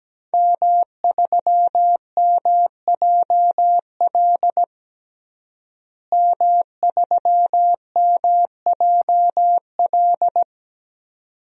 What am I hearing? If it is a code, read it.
Morse code, heard as a keyed tone, about 17 wpm, 700 Hz. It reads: M3MJL M3MJL
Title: DXCC-100 Call Signs - 5 Characters - Repeated 2x - 17wpm